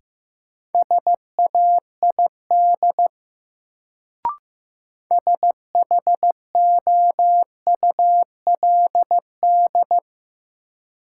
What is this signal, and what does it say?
Morse code, heard as a keyed tone, about 15 wpm, 700 Hz. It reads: SAID SHOULD
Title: Sets of 2 Words - New Word Drill - 15wpm